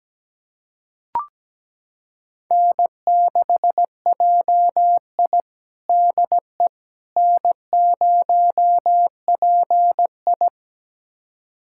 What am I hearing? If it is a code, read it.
Morse code, heard as a keyed tone, about 17 wpm, 700 Hz. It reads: N6JI DE N0PI
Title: USA Callsign DE Callsign - 4 Characters - 17wpm